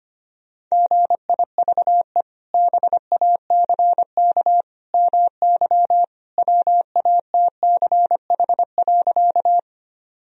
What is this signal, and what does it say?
Morse code, heard as a keyed tone, about 25 wpm, 700 Hz. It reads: GIVE BACK MY WATCH.
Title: Sentences from Top 1000 Words - Review Part 1 - 25wpm